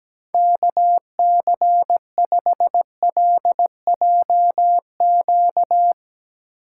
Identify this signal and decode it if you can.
Morse code, heard as a keyed tone, about 18 wpm, 700 Hz. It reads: KC5LJQ
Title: Speed Racing - USA 2x3 Call Signs 17wpm